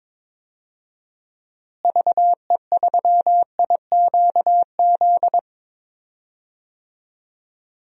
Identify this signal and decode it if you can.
Morse code, heard as a keyed tone, about 22 wpm, 700 Hz. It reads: VE3IQZ
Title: DXCC-100 Call Signs - 6 Characters - Repeated 3x - 22wpm